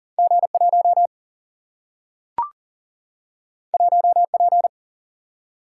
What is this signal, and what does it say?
Morse code, heard as a keyed tone, about 38 wpm, 700 Hz. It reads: G 1 1 P
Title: Single-Letter Single-Number 40wpm